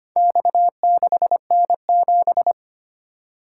Morse code, heard as a keyed tone, about 25 wpm, 700 Hz. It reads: X6N7
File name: practice / 25wpm / Four Letter-Number Combinations - 25wpm